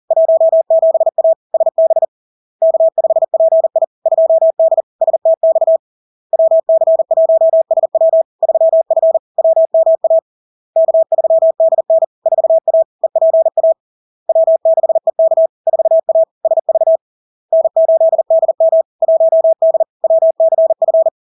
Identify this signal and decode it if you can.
Morse code, heard as a keyed tone, about 40 wpm, 650 Hz. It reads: 17A SB K5PI 2D STX WC1SW 3F WMA K3DN 4A EPA W6EK 4A SV N8DM 1D WCF
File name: LARK-FD-2022-Log-in-CW-40wpm-650-Hz